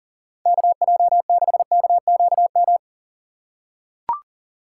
Morse code, heard as a keyed tone, about 40 wpm, 700 Hz. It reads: KJ6KQM
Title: USA Call Signs - 6 Characters - 4x Character Spacing - 40wpm